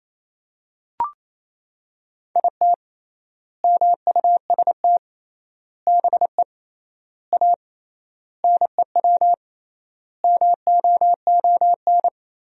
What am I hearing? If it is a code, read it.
Morse code, heard as a keyed tone, about 28 wpm, 700 Hz. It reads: IT MUST BE A NEW MOON
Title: 3x Word-Spacing - Sentences from Top 500 Words - 28wpm